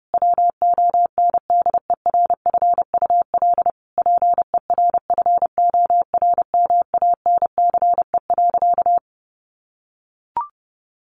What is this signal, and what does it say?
Morse code, heard as a keyed tone, about 30 wpm, 700 Hz. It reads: WONDERFUL PERFORMANCE.
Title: Sentences from Top 1100 Words 30wpm